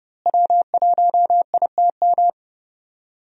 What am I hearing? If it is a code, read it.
Morse code, heard as a keyed tone, about 31 wpm, 700 Hz. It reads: W1ITM
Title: USA Call Signs - 5 Characters - Repeated 2x - 30wpm